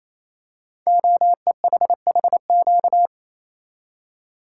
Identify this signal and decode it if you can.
Morse code, heard as a keyed tone, about 27 wpm, 700 Hz. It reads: OEHHQ
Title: Code Groups - 5 Letters - 28wpm